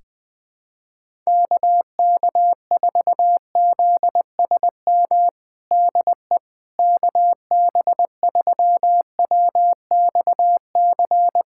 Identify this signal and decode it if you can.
Morse code, heard as a keyed tone, about 20 wpm, 700 Hz. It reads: KK4ZSM DE KB3WXC
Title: USA Callsign DE Callsign - 6 Characters - 20wpm